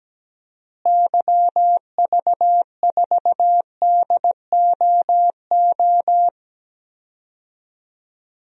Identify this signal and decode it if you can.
Morse code, heard as a keyed tone, about 17 wpm, 700 Hz. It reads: YV4DOO
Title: DXCC-100 Call Signs - 6 Characters - Repeated 3x - 17wpm